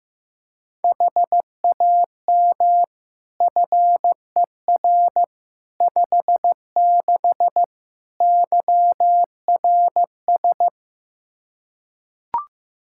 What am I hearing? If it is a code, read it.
Morse code, heard as a keyed tone, about 16 wpm, 700 Hz. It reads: HAM FER 56 YRS
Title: Practice QSO - Exchange 3 - More Info - 15wpm